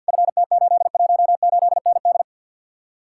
Morse code, heard as a keyed tone, about 62 wpm, 700 Hz. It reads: FT818ND
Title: Speed-Racing Common QSO Phrases - 50wpm